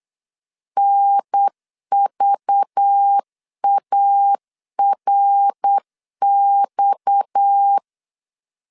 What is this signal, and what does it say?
Morse code, heard as a keyed tone, about 8 wpm, 800 Hz. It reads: NVARX
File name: CW / DET / 1 6 1500C S B 8wpm 5142